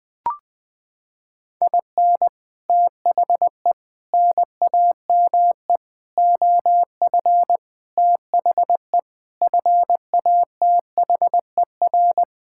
Morse code, heard as a keyed tone, about 20 wpm, 700 Hz. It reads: IN THE NAME OF THE FATHER
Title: Sets of 9 Words - 20wpm